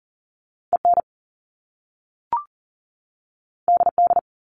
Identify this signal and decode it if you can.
Morse code, heard as a keyed tone, about 40 wpm, 700 Hz. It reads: EN DD
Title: Top Two Letter Combinations in Words 40wpm